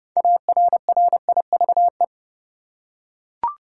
Morse code, heard as a keyed tone, about 30 wpm, 700 Hz. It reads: ARRIVE
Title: Six Letter Words - 30wpm